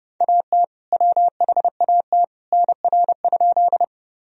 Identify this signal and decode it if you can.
Morse code, heard as a keyed tone, about 30 wpm, 700 Hz. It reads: AT WHAT NR?
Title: Sentences from Top 100 Words and Abbreviations 30wpm